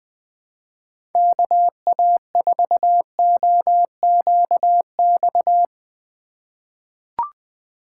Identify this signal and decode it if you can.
Morse code, heard as a keyed tone, about 20 wpm, 700 Hz. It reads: KA4OQX
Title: USA Call Signs - 6 Characters - 3x Character Spacing - 20wpm